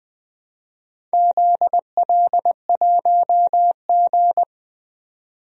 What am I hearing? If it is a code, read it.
Morse code, heard as a keyed tone, about 20 wpm, 700 Hz. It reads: ZL1G
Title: DXCC-100 Call Signs - 4 Characters - 20wpm